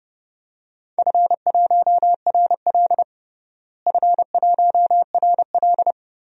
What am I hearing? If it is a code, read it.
Morse code, heard as a keyed tone, about 30 wpm, 700 Hz. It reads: F1RL F1RL
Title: DXCC-100 Call Signs - 4 Characters - Repeated 2x - 30wpm